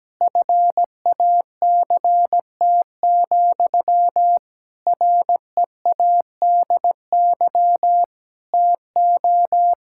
Morse code, heard as a keyed tone, about 17 wpm, 700 Hz. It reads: FACT, READY TO
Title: Sentences from Top 1000 Words - Review Part 5 - 17wpm